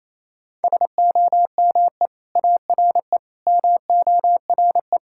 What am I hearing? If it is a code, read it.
Morse code, heard as a keyed tone, about 28 wpm, 700 Hz. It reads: SOME ARE MORE
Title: Sentences from Top 1000 Words - Review Part 6 - 28wpm